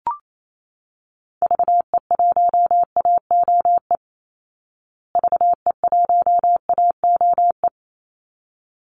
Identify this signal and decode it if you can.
Morse code, heard as a keyed tone, about 28 wpm, 700 Hz. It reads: VE1AOE VE1AOE
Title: Call Signs Repeated 3x - 28wpm